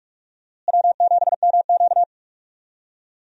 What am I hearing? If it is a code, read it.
Morse code, heard as a keyed tone, about 41 wpm, 700 Hz. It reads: W7MQ
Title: USA Call Signs - 4 Characters - Repeated 3x - 45wpm